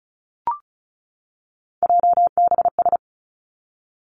Morse code, heard as a keyed tone, about 36 wpm, 700 Hz. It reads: JBS
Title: Three Challenging Letter-Number Combinations 35wpm